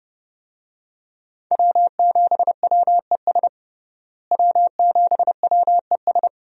Morse code, heard as a keyed tone, about 32 wpm, 700 Hz. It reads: W7WES W7WES
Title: USA Call Signs - 5 Characters - Repeated 2x - 30wpm